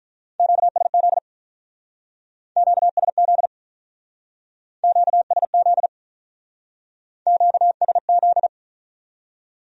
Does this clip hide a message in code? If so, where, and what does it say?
Yes: Morse code from 0.4 to 8.5 s, heard as a keyed tone, about 46 wpm, 700 Hz. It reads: QSZ QSZ QSZ QSZ